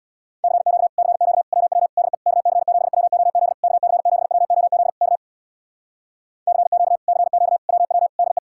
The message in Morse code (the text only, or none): CQ CQ WW DE KC7WOZ KC7WOZ K CQ CQ WW DE